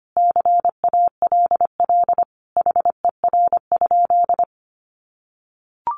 Morse code, heard as a keyed tone, about 25 wpm, 700 Hz. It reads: CALL HER?
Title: Sentences from Top 100 Words 25wpm